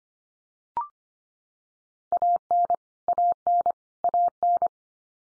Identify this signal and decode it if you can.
Morse code, heard as a keyed tone, about 25 wpm, 700 Hz. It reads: AN AN AN
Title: Top 100 Words 25wpm x3